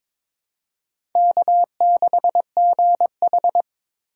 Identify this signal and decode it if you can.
Morse code, heard as a keyed tone, about 22 wpm, 700 Hz. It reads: K6GH
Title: USA Call Signs - 4x Character Spacing - 22wpm